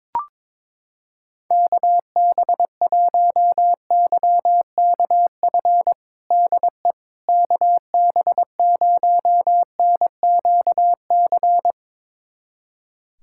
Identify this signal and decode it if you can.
Morse code, heard as a keyed tone, about 22 wpm, 700 Hz. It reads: KB1YKF DE KB0NQC
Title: USA Callsign DE Callsign - 6 Characters - 22wpm